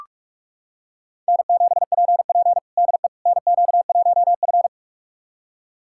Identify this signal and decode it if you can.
Morse code, heard as a keyed tone, about 41 wpm, 700 Hz. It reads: N7PP DE NQ1F
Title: USA Callsign DE Callsign - 4 Characters - 45wpm